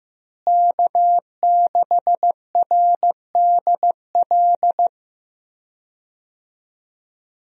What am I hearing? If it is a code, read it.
Morse code, heard as a keyed tone, about 15 wpm, 700 Hz. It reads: K6RDL
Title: USA Call Signs - 5 Characters - Repeated 3x - 15wpm